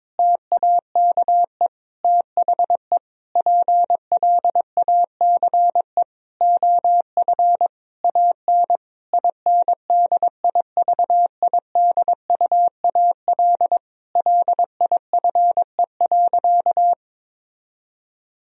Morse code, heard as a keyed tone, about 22 wpm, 700 Hz. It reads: TAKE THE PLACE OF AN INDIVIDUAL LIFE.
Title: Sentences from Top 1000 Words - Review Part 4 - 22wpm